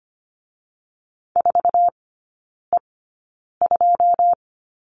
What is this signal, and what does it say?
Morse code, heard as a keyed tone, about 25 wpm, spaced as for 9 wpm, 690 Hz. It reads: 4E2
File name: Lesson 14e - Number 2 - Three Characters 25wpm